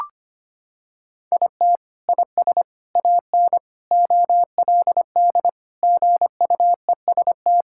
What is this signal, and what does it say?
Morse code, heard as a keyed tone, about 25 wpm, 700 Hz. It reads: IT IS AN OLD GUEST
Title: Sentences from Top 1600 Words 25wpm